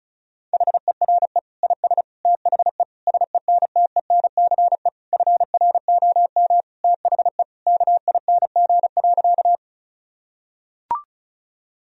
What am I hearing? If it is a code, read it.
Morse code, heard as a keyed tone, about 36 wpm, 700 Hz. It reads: HERE IS THE SENTENCE FROM THE KING.
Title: Sentences from Top 1000 Words - Review Part 1 - 35wpm